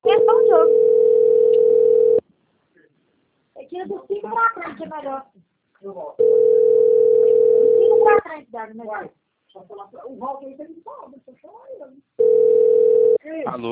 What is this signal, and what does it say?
Ringback tone (North America), 440 + 480 Hz, with speech.
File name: out- - - - - .43015